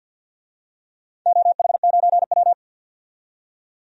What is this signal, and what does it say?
Morse code, heard as a keyed tone, about 51 wpm, 700 Hz. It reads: OH9W
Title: DXCC-100 Call Signs - 4 Characters - 50wpm